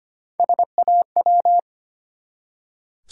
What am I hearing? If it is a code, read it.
Morse code, heard as a keyed tone, about 25 wpm, 690 Hz. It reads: SAW
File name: practice / 25wpm / Easy Words with Hints - 25wpm